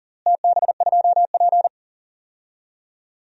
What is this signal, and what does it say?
Morse code, heard as a keyed tone, about 40 wpm, 700 Hz. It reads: TB2P
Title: DXCC-100 Call Signs - 4 Characters - Repeated 2x - 40wpm